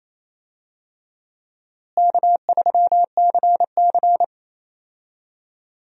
Morse code, heard as a keyed tone, about 27 wpm, 700 Hz. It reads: K3CC
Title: USA Call Signs - 4 Characters - Repeated 2x - 28wpm